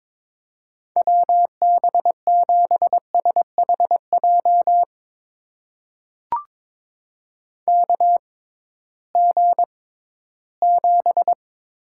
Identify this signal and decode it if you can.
Morse code, heard as a keyed tone, about 21 wpm, spaced as for 14 wpm, 700 Hz. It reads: WB7SHJ KG7
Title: USA Call Signs - 6 Characters - 6x Character Spacing - 22wpm